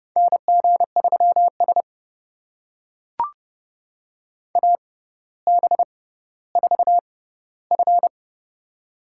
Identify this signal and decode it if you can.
Morse code, heard as a keyed tone, about 30 wpm, spaced as for 15 wpm, 700 Hz. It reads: NG3S AB4F
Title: USA Call Signs - 4 Characters - 6x Character Spacing - 30wpm